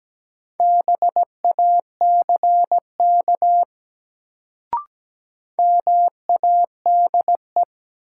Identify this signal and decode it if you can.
Morse code, heard as a keyed tone, about 17 wpm, 700 Hz. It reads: BACK MADE
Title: Sentences from Top 200 Words - New Word Drill - 17wpm